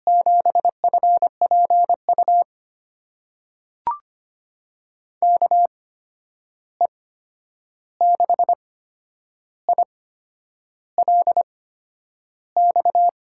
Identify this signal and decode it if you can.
Morse code, heard as a keyed tone, about 25 wpm, spaced as for 9 wpm, 700 Hz. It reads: 7FPU KE6ILX
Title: USA Call Signs - 6 Characters - 8x Character Spacing - 25wpm